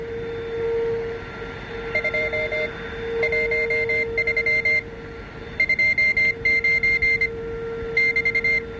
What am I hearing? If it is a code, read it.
Morse code, heard as a keyed tone, about 25 wpm, spaced as for 18 wpm, 2100 Hz, with noise. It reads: 213 29 <BT>